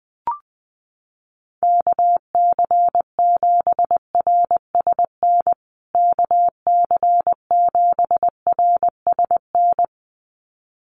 Morse code, heard as a keyed tone, about 20 wpm, 700 Hz. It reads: KC7RSN KC7RSN
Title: USA Call Signs Repeated 2x 20wpm